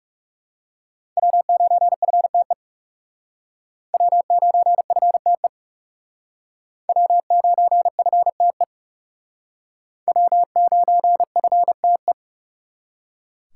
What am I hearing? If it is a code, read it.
Morse code, heard as a keyed tone, about 39 wpm, 700 Hz. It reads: W9FTE W9FTE W9FTE W9FTE